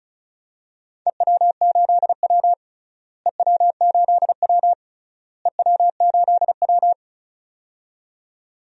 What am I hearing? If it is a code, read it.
Morse code, heard as a keyed tone, about 34 wpm, 700 Hz. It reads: EW8W EW8W EW8W
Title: DXCC-100 Call Signs - 4 Characters - Repeated 3x - 35wpm